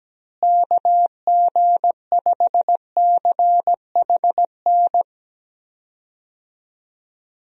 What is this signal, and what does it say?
Morse code, heard as a keyed tone, about 17 wpm, 700 Hz. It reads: KG5CHN
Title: USA Call Signs - 6 Characters - Repeated 3x - 17wpm